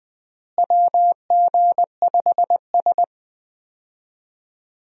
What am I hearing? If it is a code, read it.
Morse code, heard as a keyed tone, about 20 wpm, 700 Hz. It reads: WG5S
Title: USA Call Signs - 4 Characters - Repeated 2x - 20wpm